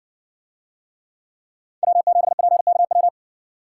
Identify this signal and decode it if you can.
Morse code, heard as a keyed tone, about 56 wpm, 700 Hz. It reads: W7PKW